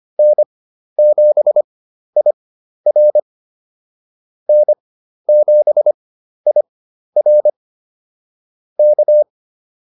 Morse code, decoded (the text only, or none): N7IR N7IR K